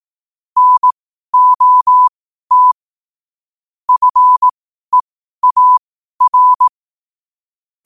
Morse code, heard as a keyed tone, about 18 wpm, spaced as for 12 wpm, 1000 Hz. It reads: NOT FEAR
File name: qotdfiles / 12wpm